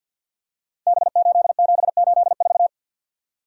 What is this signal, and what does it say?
Morse code, heard as a keyed tone, about 52 wpm, 700 Hz. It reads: B8784
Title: Code Groups - 5 Letter:Numbers - 50wpm